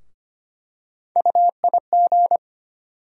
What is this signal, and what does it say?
Morse code, heard as a keyed tone, about 25 wpm, 700 Hz. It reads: UIG
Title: USA Call Sign Postfixes - 3 Letters 25wpm